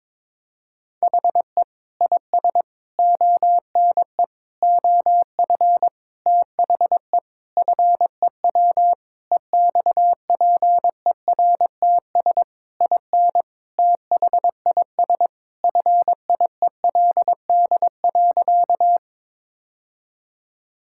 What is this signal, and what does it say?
Morse code, heard as a keyed tone, about 22 wpm, 700 Hz. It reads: HE IS ONE OF THE FEW EXPERTS IN THIS FIELD.
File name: practice / 22wpm / Sentences with 4-Word Phrases - 22wpm